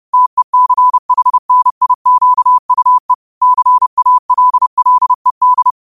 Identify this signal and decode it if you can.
Morse code, heard as a keyed tone, about 30 wpm, 1000 Hz. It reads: TECHNIQUE CALLED
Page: Quotes for Mon, 28 Apr 2025 in Morse Code at 30 words per minute.